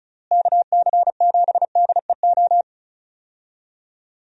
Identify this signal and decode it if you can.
Morse code, heard as a keyed tone, about 36 wpm, 700 Hz. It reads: KC7DEO
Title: USA Call Signs - 6 Characters - Repeated 2x - 35wpm